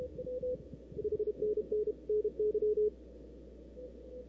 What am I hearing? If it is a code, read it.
Morse code, heard as a keyed tone, about 32 wpm, 420 Hz. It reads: H NN NY
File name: CW / 2005-k5tr arrl cw dx -03:00